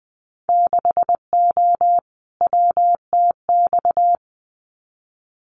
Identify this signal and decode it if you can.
Morse code, heard as a keyed tone, about 20 wpm, 700 Hz. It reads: 6O WTX
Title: Winter Field Day Exchanges Encore 20wpm